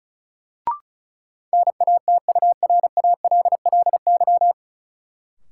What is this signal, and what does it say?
Morse code, heard as a keyed tone, about 36 wpm, 700 Hz. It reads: NATURALLY
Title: Sentences from Top 1300 Words - New Word Drill Alt - 35wpm